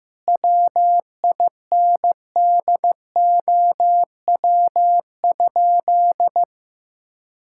Morse code, heard as a keyed tone, about 15 wpm, 700 Hz. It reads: WINDOW?
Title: Sentences from Top 1600 Words 15wpm